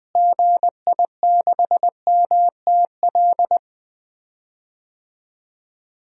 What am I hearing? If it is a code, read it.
Morse code, heard as a keyed tone, about 21 wpm, 700 Hz. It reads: GI6MTL
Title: DXCC-100 Call Signs - 6 Characters - Repeated 3x - 20wpm